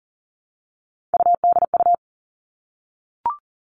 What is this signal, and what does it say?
Morse code, heard as a keyed tone, about 40 wpm, 700 Hz. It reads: UDU
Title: Three Challenging Letter-Number Combinations 40wpm